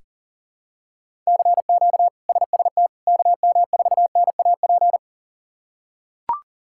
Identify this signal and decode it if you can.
Morse code, heard as a keyed tone, about 40 wpm, 700 Hz. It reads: CQ SST KM4NAP
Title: K1USN SST QSO Practice - 40wpm